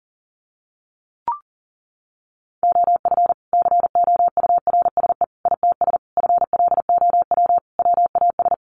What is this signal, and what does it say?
Morse code, heard as a keyed tone, about 40 wpm, 700 Hz. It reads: OF COURSE ITS FLOW WAS
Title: Sentences from Top 1200 Words 40wpm